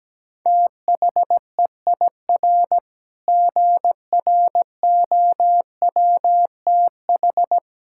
Morse code, heard as a keyed tone, about 17 wpm, 700 Hz. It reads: THEIR GROWTH
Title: Sentences from Top 1000 Words - Review Part 5 - 17wpm